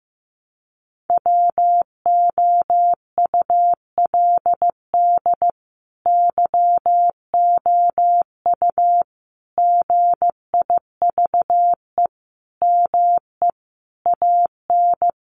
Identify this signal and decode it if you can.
Morse code, heard as a keyed tone, about 15 wpm, 700 Hz. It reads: WOULD YOU GIVE ME AN
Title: Sentences from Top 1000 Words 15wpm